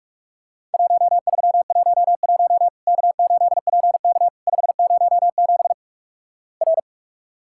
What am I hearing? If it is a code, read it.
Morse code, heard as a keyed tone, about 48 wpm, 700 Hz. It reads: 1311 K8PK 507 R
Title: Worked All Europe DX Contest - QTC Exchange - 45wpm